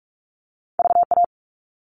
Morse code, heard as a keyed tone, about 46 wpm, 700 Hz. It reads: VA
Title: All Cut Numbers - 2-Digit 45wpm